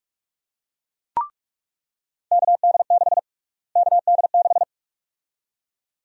Morse code, heard as a keyed tone, about 46 wpm, 700 Hz. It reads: KD6 KD6
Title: USA Call Sign Prefixes - Repeated 2x - 45wpm